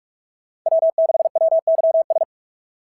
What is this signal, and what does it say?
Morse code, heard as a keyed tone, about 44 wpm, 690 Hz. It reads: W6WYS
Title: SOTA QSO Practice 45wpm